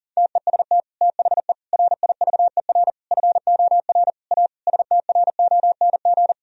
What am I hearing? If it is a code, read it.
Morse code, heard as a keyed tone, about 37 wpm, 700 Hz. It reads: TEST THE RIVER FOR A STRONG
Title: Sentences from Top 1000 Words - Review Part 4 - 40wpm